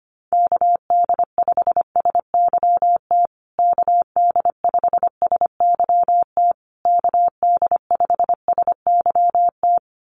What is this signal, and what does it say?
Morse code, heard as a keyed tone, about 25 wpm, 700 Hz. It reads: KD5SYT KD5SYT KD5SYT
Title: USA Call Signs Repeated 3x 25wpm